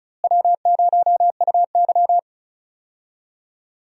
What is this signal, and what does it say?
Morse code, heard as a keyed tone, about 36 wpm, 700 Hz. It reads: W0UY
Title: USA Call Signs - 4 Characters - Repeated 3x - 35wpm